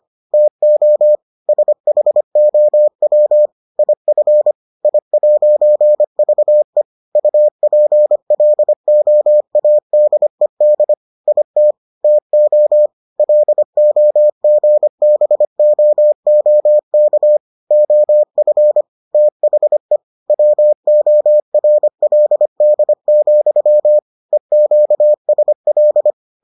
Morse code, heard as a keyed tone, about 25 wpm, 600 Hz. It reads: TO SHOW IF I'VE UPLOADED IT TO LOGBOOK OF THE WORLD, EQSL